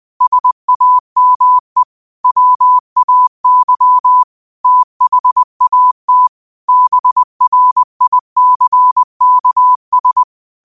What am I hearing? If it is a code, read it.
Morse code, heard as a keyed tone, about 20 wpm, 1000 Hz. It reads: SAME WAY THAT BRICKS
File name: qotdfiles / 20wpm